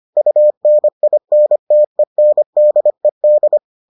Morse code, heard as a keyed tone, about 25 wpm, 600 Hz. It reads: UNINTENDED